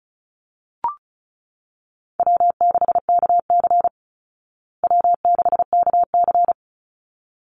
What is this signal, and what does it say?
Morse code, heard as a keyed tone, about 34 wpm, 700 Hz. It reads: W6KC W6KC
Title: Call Signs Repeated 3x - 35wpm